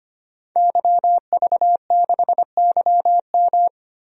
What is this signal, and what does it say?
Morse code, heard as a keyed tone, about 25 wpm, 700 Hz. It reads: YV6YM
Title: DXCC-100 Call Signs - 5 Characters - 25wpm